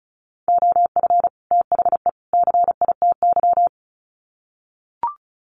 Morse code, heard as a keyed tone, about 34 wpm, 700 Hz. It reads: OF THE CITY
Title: Sets of 3 Words 35wpm